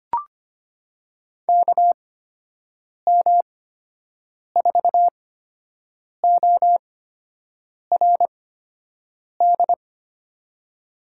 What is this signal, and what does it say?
Morse code, heard as a keyed tone, about 25 wpm, spaced as for 7 wpm, 700 Hz. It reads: KM4ORD
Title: USA Call Signs - 8x Character Spacing - 25wpm